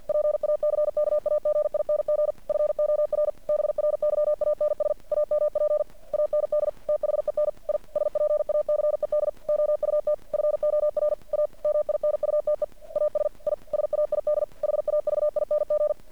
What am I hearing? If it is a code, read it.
Morse code, heard as a keyed tone, about 50 wpm, 600 Hz. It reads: PACKAGING FOR BRYANS AMP AND THEN I SPACED OUT FOR A MINUTE AS I SAID HAVING
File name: archived / CRSnet